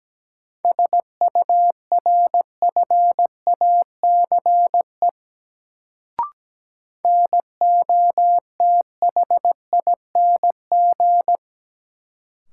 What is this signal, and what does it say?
Morse code, heard as a keyed tone, about 17 wpm, 700 Hz. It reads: SURFACE NOTHING
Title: Sets of 4 Words - New Word Drill - 17wpm